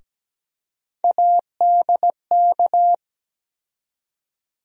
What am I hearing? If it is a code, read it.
Morse code, heard as a keyed tone, about 17 wpm, 700 Hz. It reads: ADK
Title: USA Call Sign Postfixes - 3 Letters 17wpm